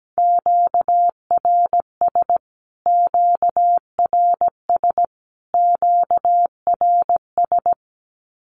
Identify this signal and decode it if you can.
Morse code, heard as a keyed tone, about 17 wpm, 700 Hz. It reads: QRS QRS QRS
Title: Q-Codes Repeated 3x 17wpm